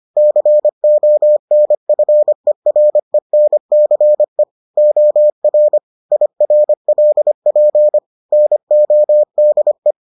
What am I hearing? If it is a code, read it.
Morse code, heard as a keyed tone, about 25 wpm, 600 Hz. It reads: CONFERENCE OR IRLP NODE